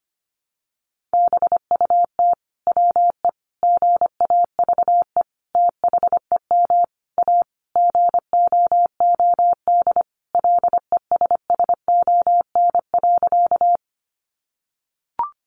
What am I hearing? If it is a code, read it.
Morse code, heard as a keyed tone, about 25 wpm, 700 Hz. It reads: BUT WE GAVE THEM A GOOD LESSON.
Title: Sentences from Top 1200 Words 25wpm